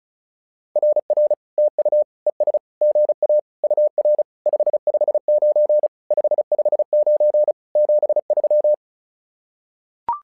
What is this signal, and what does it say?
Morse code, heard as a keyed tone, about 36 wpm, 600 Hz. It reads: RR TU ES GA UR 559 559 73
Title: SOTA QSO Practice 35wpm